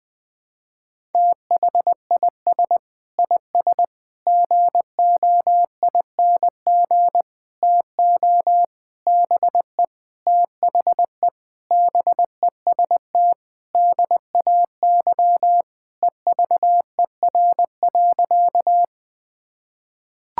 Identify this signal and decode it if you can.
Morse code, heard as a keyed tone, about 21 wpm, 700 Hz. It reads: THIS IS GOING TO BE THE BEST DAY EVER.
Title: Sentences with 5-Word Phrases - 20wpm